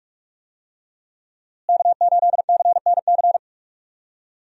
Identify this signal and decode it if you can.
Morse code, heard as a keyed tone, about 43 wpm, 700 Hz. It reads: K8CNC
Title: USA Call Signs - 5 Characters - Repeated 3x - 45wpm